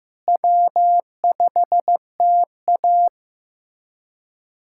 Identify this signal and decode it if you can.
Morse code, heard as a keyed tone, about 15 wpm, 700 Hz. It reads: W5TA
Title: USA Callsign DE Callsign - 4 Characters - 15wpm